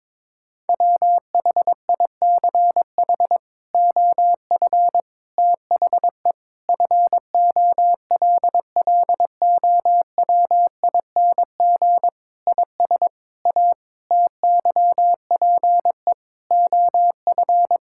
Morse code, heard as a keyed tone, about 22 wpm, 700 Hz. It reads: WHICH OF THE FOLLOWING IS A TYPE OF
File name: practice / 22wpm / Sets of 8 Words - 22wpm